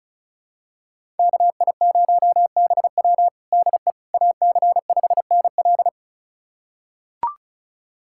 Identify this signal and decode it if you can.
Morse code, heard as a keyed tone, about 34 wpm, 700 Hz. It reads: KI0BW DE AC5NL
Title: USA Callsign DE Callsign - 5 Characters - 35wpm